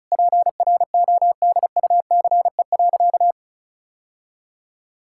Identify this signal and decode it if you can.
Morse code, heard as a keyed tone, about 34 wpm, 700 Hz. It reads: PRODUCE.
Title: Sentences from Top 1000 Words - Review Part 5 - 35wpm